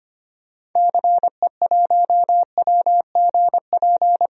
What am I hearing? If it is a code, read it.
Morse code, heard as a keyed tone, about 25 wpm, 690 Hz. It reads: CE1WGP
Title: DXCC-100 Call Signs - 6 Characters - Repeated 3x - 25wpm